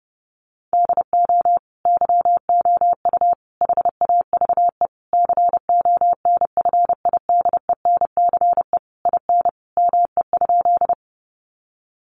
Morse code, heard as a keyed tone, about 30 wpm, 700 Hz. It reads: DO YOU HAVE CONFIDENCE IN ME?
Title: Sentences from Top 1500 Words 30wpm